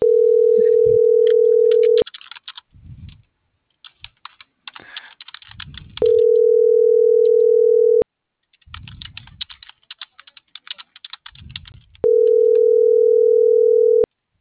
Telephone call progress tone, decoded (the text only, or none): ringback tone